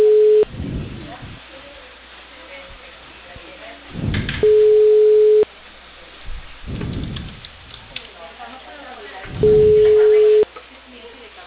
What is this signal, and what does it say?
Ringback tone (Europe, 425 Hz), with speech.